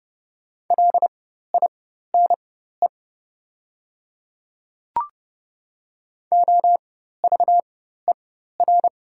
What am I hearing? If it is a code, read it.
Morse code, heard as a keyed tone, about 30 wpm, spaced as for 14 wpm, 690 Hz. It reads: LINE OVER
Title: 4x Character Spacing - Top 200 Words - 30wpm